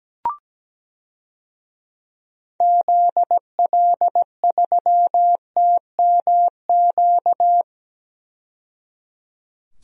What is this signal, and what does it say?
Morse code, heard as a keyed tone, about 17 wpm, 700 Hz. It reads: ZL3TMQ
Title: DXCC-100 Call Signs - 6 Characters - 17wpm